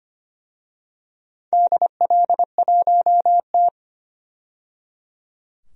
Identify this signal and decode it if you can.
Morse code, heard as a keyed tone, about 25 wpm, 700 Hz. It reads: DL1T
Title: DXCC-100 Call Signs - 4 Characters - 25wpm